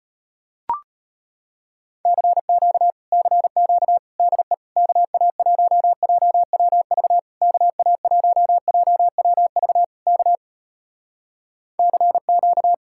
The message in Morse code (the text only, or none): CQ CQ DE KA1JWV KA1JWV K CQ